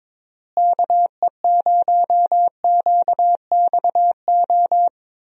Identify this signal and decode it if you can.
Morse code, heard as a keyed tone, about 22 wpm, 700 Hz. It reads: KE0QXO
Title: USA Call Signs - 6 Characters - Repeated 3x - 22wpm